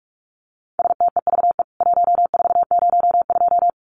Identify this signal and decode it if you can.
Morse code, heard as a keyed tone, about 44 wpm, 700 Hz. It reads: STEVE 1402
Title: CWT Exchange 45wpm